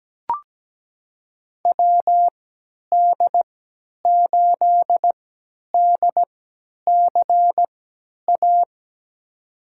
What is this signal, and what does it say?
Morse code, heard as a keyed tone, about 17 wpm, spaced as for 10 wpm, 700 Hz. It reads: WD8DCA
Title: USA Call Signs - 3x Character Spacing - 17wpm